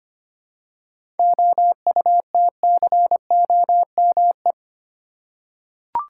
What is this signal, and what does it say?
Morse code, heard as a keyed tone, about 25 wpm, 700 Hz. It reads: OUTCOME
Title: Seven Letter Words - 25wpm